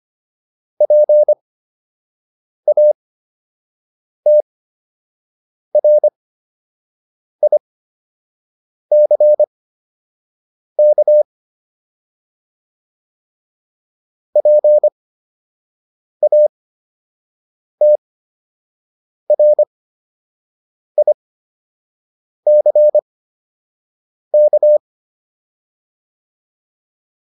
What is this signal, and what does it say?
Morse code, heard as a keyed tone, about 25 wpm, spaced as for 6 wpm, 600 Hz. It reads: PATRICK PATRICK
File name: QSO 105 06